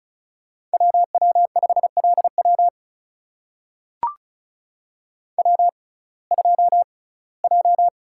Morse code, heard as a keyed tone, about 36 wpm, spaced as for 21 wpm, 700 Hz. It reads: WW5LW W2J
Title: USA Call Signs - 5 Characters - 6x Character Spacing - 35wpm